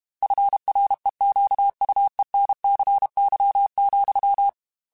Morse code, heard as a keyed tone, about 32 wpm, 800 Hz. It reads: FREQUENCY,
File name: Technical Subjects / Technical Subjects-mp3s / XL @30